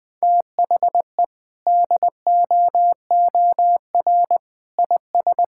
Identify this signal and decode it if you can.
Morse code, heard as a keyed tone, about 20 wpm, 700 Hz. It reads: THE DOOR IS
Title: Sentences from Top 1600 Words 20wpm